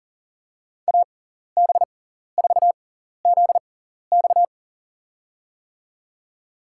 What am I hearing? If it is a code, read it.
Morse code, heard as a keyed tone, about 40 wpm, spaced as for 14 wpm, 700 Hz. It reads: AB4ZX
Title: USA Call Signs - 5 Characters - 6x Character Spacing - 40wpm